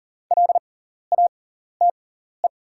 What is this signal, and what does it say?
Morse code, heard as a keyed tone, about 40 wpm, spaced as for 14 wpm, 700 Hz. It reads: LATE
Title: 6x Character Spacing - Top 300 Words - 40wpm